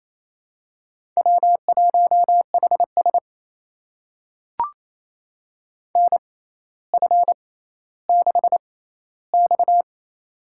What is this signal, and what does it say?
Morse code, heard as a keyed tone, about 28 wpm, spaced as for 14 wpm, 700 Hz. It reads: W1HS NF6X
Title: USA Call Signs - 4 Characters - 6x Character Spacing - 28wpm